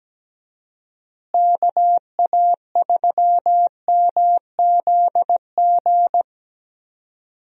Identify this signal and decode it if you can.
Morse code, heard as a keyed tone, about 17 wpm, 700 Hz. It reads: KA3MZG
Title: USA Call Signs - 6 Characters - Repeated 3x - 17wpm